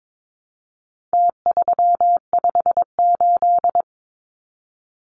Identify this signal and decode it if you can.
Morse code, heard as a keyed tone, about 22 wpm, 700 Hz. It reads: T358
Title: Numbers - Contest Style - 4-Digit 22wpm